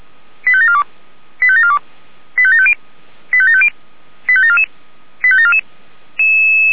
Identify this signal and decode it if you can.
Five-tone selective call (ZVEI), heard as quick runs of five tones, tones of about 70 ms. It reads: 85742 85742 85750 85750 85744 85744
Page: audio.f85360.feuerwehr-orlamuende.org